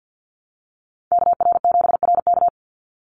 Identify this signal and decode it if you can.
Morse code, heard as a keyed tone, about 50 wpm, 700 Hz. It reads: KR7RK
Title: NA SOTA Activator Callsigns 50wpm